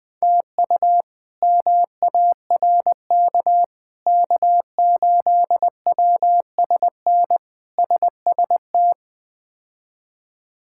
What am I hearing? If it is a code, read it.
Morse code, heard as a keyed tone, about 20 wpm, 700 Hz. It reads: TU MARK K8WSN SST
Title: K1USN SST QSO Practice - 20wpm